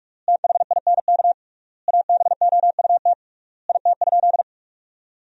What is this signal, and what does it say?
Morse code, heard as a keyed tone, about 43 wpm, 700 Hz. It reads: THINK ABOUT IT?
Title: 3x Word-Spacing - Sentences from Top 200 Words - 45wpm